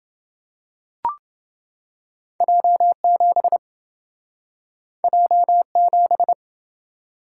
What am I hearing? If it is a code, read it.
Morse code, heard as a keyed tone, about 27 wpm, 700 Hz. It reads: J7 J7